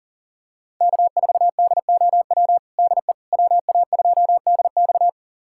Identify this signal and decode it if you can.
Morse code, heard as a keyed tone, about 40 wpm, 700 Hz. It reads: K4DOW DE WA2DX
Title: USA Callsign DE Callsign - 5 Characters - 40wpm